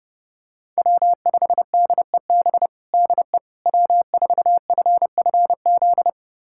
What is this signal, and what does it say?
Morse code, heard as a keyed tone, about 32 wpm, 700 Hz. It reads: W5DEB DE W4FFZ
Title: USA Callsign DE Callsign - 5 Characters - 30wpm